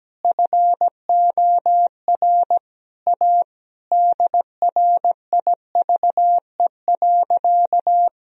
Morse code, heard as a keyed tone, about 17 wpm, 700 Hz. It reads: FOR A DRIVE.
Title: Sentences from Top 1000 Words - Review Part 2 - 17wpm